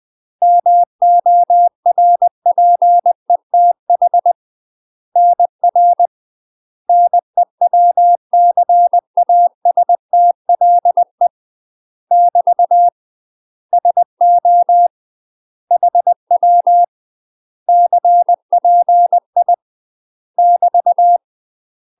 Morse code, heard as a keyed tone, about 20 wpm, 700 Hz. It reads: MORPETH NR NEWCASTLE <BT> SO HW CPI <BT>
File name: UK MASTER QSOs TRAINING / 20WPM / UK MASTER QSOs 0018